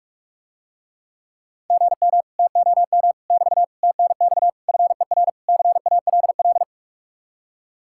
Morse code, heard as a keyed tone, about 49 wpm, 700 Hz. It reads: GM TOM <BT> TNX FER CALL